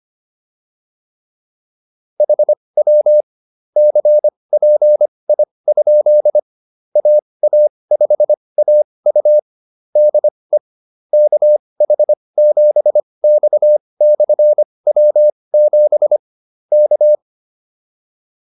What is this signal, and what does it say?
Morse code, heard as a keyed tone, about 25 wpm, spaced as for 20 wpm, 600 Hz. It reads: HW CPI? AA5AU DE KH7X/W7 K
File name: QSO104 20